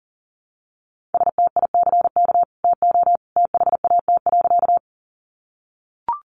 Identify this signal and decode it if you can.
Morse code, heard as a keyed tone, about 40 wpm, 700 Hz. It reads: STICK TO THAT.
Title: Sentences from Top 1200 Words 40wpm